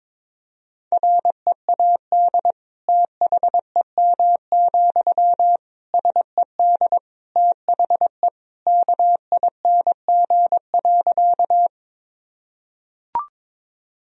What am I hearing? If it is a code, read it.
Morse code, heard as a keyed tone, about 22 wpm, 700 Hz. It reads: READ THEM, SED THE KING.
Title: Sentences from Top 400 Words and Abbreviations 22wpm